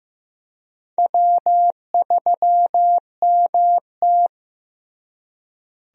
Morse code, heard as a keyed tone, about 15 wpm, 700 Hz. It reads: W3MT
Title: USA Call Signs - 4 Characters - Repeated 3x - 15wpm